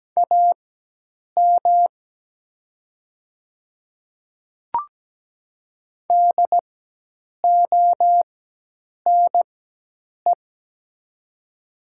Morse code, heard as a keyed tone, about 17 wpm, spaced as for 8 wpm, 700 Hz. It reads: AM DONE
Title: 4x Character Spacing - Top 500 Words - 17wpm